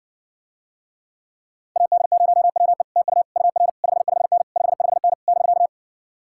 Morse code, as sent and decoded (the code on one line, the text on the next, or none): .- -.. ----- .-- . / - ..- / ..- .-. / ..... ..... -. / ..... ..... -. / -...-.-
AD0WE TU UR 55N 55N <BK>